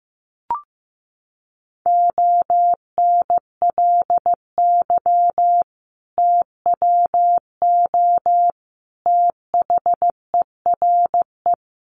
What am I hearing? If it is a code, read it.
Morse code, heard as a keyed tone, about 15 wpm, 700 Hz. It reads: ONLY TWO THERE
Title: Sentences from Top 200 Words 15wpm